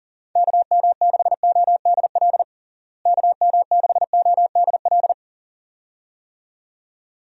Morse code, heard as a keyed tone, about 40 wpm, 700 Hz. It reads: KM6ODL KM6ODL
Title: USA Call Signs - 6 Characters - Repeated 2x - 40wpm